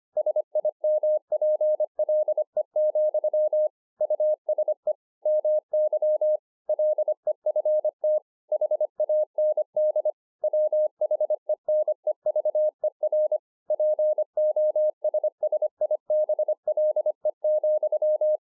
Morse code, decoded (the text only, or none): SIMPLE, USE MY LEFT HAND WHENEVER POSSIBLE,